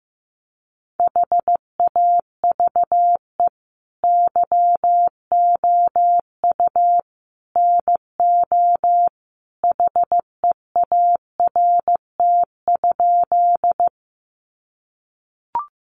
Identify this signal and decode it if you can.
Morse code, heard as a keyed tone, about 15 wpm, 700 Hz. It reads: HAVE YOU NO HEART?
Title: Sentences from Top 1200 Words 15wpm